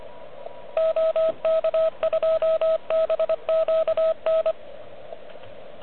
Morse code, heard as a keyed tone, about 25 wpm, 640 Hz, with radio noise. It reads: OK2BQN
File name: OK QRP 2022 / Zvuk / REC30